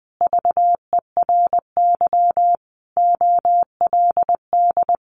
Morse code, heard as a keyed tone, about 20 wpm, 700 Hz. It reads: VERY OLD
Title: A Sound Of Thunder 20wpm